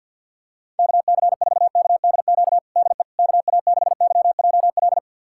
Morse code, heard as a keyed tone, about 48 wpm, 700 Hz. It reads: KC4KDQ DE KA6YJL
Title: USA Callsign DE Callsign - 6 Characters - 50wpm